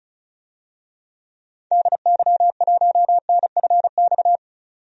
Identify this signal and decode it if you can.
Morse code, heard as a keyed tone, about 36 wpm, 700 Hz. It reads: DY1NFX
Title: DXCC-100 Call Signs - 6 Characters - 35wpm